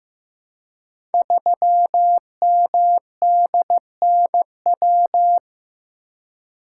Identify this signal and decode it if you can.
Morse code, heard as a keyed tone, about 15 wpm, 700 Hz. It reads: 3MDNW
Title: Code Groups - 5 Letter:Numbers - 15wpm